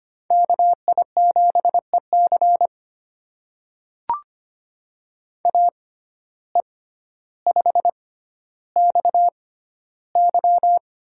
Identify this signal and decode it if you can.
Morse code, heard as a keyed tone, about 25 wpm, spaced as for 13 wpm, 700 Hz. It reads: KI7EC AE5XY
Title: USA Call Signs - 5 Characters - 6x Character Spacing - 25wpm